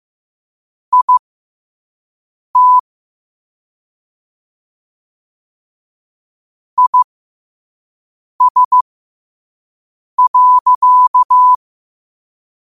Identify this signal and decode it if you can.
Morse code, heard as a keyed tone, about 15 wpm, spaced as for 5 wpm, 1000 Hz. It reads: IT IS.
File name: qotdfiles / 5wpm